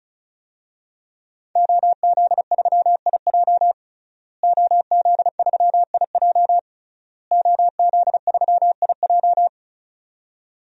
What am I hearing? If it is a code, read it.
Morse code, heard as a keyed tone, about 36 wpm, 700 Hz. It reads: OZ3IJ OZ3IJ OZ3IJ
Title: DXCC-100 Call Signs - 5 Characters - Repeated 3x - 35wpm